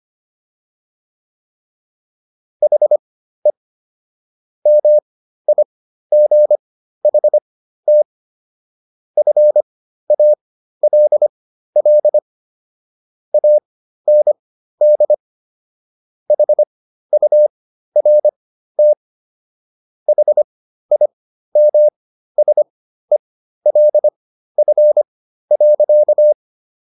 Morse code, heard as a keyed tone, about 25 wpm, spaced as for 13 wpm, 600 Hz. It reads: HE MIGHT FALL AND HURT HIMSELF.